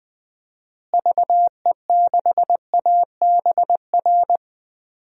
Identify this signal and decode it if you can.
Morse code, heard as a keyed tone, about 20 wpm, 700 Hz. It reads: VE6ABR
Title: DXCC-100 Call Signs - 6 Characters - Repeated 2x - 20wpm